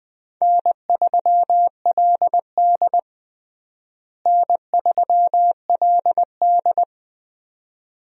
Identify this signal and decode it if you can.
Morse code, heard as a keyed tone, about 20 wpm, 700 Hz. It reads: N3LD N3LD
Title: USA Call Signs - 4 Characters - Repeated 2x - 20wpm